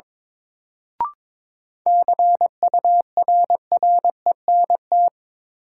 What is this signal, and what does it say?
Morse code, heard as a keyed tone, about 22 wpm, 700 Hz. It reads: CURRENT
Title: Sentences from Top 1000 Words - Review Part 4 - New Word Drill - 22wpm